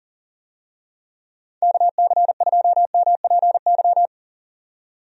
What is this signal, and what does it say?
Morse code, heard as a keyed tone, about 37 wpm, 700 Hz. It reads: KC2MPY
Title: USA Call Signs - 6 Characters - 40wpm